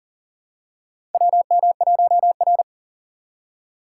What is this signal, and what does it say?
Morse code, heard as a keyed tone, about 40 wpm, 700 Hz. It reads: WM1R
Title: USA Call Signs - 4 Characters - 4x Character Spacing - 40wpm